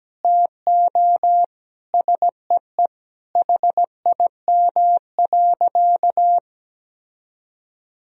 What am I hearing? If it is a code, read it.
Morse code, heard as a keyed tone, about 17 wpm, 700 Hz. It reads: TO SEE HIM.
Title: Sentences from Top 1000 Words - Review Part 2 - 17wpm